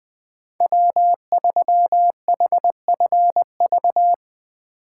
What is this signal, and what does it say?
Morse code, heard as a keyed tone, about 20 wpm, 700 Hz. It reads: W3HFV
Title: USA Call Signs - 5 Characters - 8x Character Spacing - 20wpm